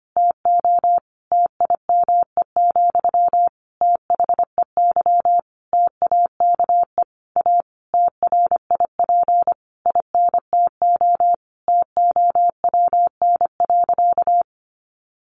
Morse code, heard as a keyed tone, about 25 wpm, 700 Hz. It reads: TO TIME, THEY TAKE A TRIP INTO TOWN.
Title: Sentences from Top 1300 Words 25wpm